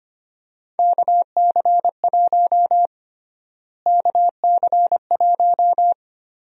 Morse code, heard as a keyed tone, about 25 wpm, 700 Hz. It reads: KC1 KC1
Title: USA Call Sign Prefixes - Repeated 2x - 25wpm